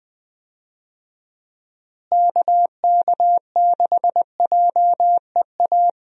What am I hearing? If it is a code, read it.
Morse code, heard as a keyed tone, about 20 wpm, 700 Hz. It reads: KK6JEA
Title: USA Call Signs - 6 Characters - Repeated 2x - 20wpm